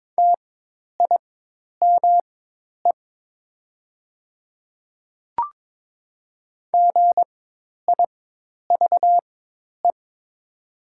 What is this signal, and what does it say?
Morse code, heard as a keyed tone, about 22 wpm, spaced as for 10 wpm, 700 Hz. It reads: TIME GIVE
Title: 4x Character Spacing - Top 200 Words - 22wpm